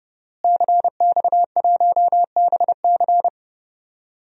Morse code, heard as a keyed tone, about 31 wpm, 700 Hz. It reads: CX1BC
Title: DXCC-100 Call Signs - 5 Characters - Repeated 2x - 30wpm